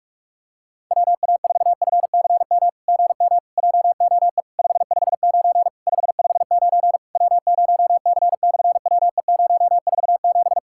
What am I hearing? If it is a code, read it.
Morse code, heard as a keyed tone, about 46 wpm, 700 Hz. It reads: WA4FCM GM JOE 559 559 W0C/WE047
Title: SOTA QSO Practice 45wpm